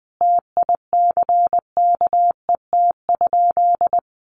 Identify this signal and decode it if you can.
Morse code, heard as a keyed tone, about 20 wpm, 700 Hz. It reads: TICKET?
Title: Sentences from Top 1400 Words 20wpm